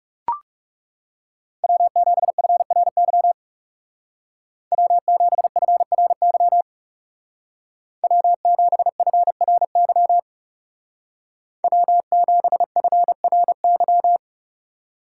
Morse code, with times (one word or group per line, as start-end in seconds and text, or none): 1.6-3.3 W7FRY
4.7-6.6 W7FRY
8.0-10.2 W7FRY
11.6-14.2 W7FRY